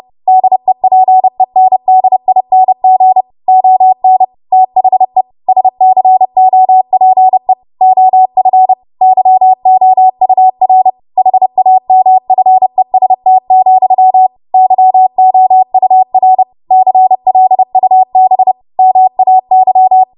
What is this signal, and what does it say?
Morse code, heard as a keyed tone, about 30 wpm, 750 Hz. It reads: DEPENDING ON THE SCOPE OF YOUR HAMFEST, YOUR CLUB MAY
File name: Morse / 30WPM